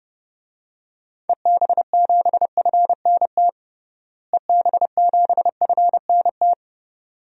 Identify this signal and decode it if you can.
Morse code, heard as a keyed tone, about 30 wpm, 700 Hz. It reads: EB7FNT EB7FNT
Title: DXCC-100 Call Signs - 6 Characters - Repeated 2x - 30wpm